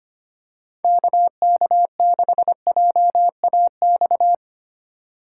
Morse code, heard as a keyed tone, about 25 wpm, 700 Hz. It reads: KK6JAX
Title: USA Call Signs - 2x Character Spacing - 25wpm